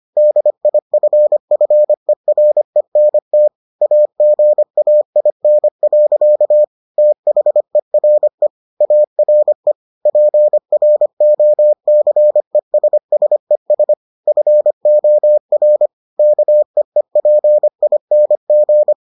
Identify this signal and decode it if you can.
Morse code, heard as a keyed tone, about 25 wpm, 600 Hz. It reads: DIFFERENT AGAIN. THERE ARE PROCESSES FOR KEEPING